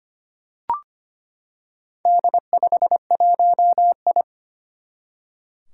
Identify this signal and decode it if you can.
Morse code, heard as a keyed tone, about 25 wpm, 700 Hz. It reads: D51I
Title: Four Challenging Letter-Number Combinations 25wpm